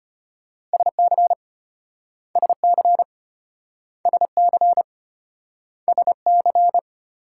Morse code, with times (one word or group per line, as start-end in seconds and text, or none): 0.7-1.3 SC
2.4-3.0 SC
4.1-4.8 SC
5.9-6.8 SC